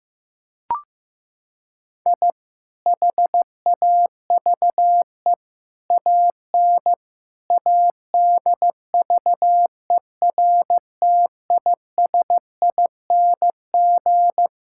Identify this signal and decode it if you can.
Morse code, heard as a keyed tone, about 15 wpm, 700 Hz. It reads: I HAVE AN ADVERTISING
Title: Sentences from Top 1000 Words - Review Part 6 - 15wpm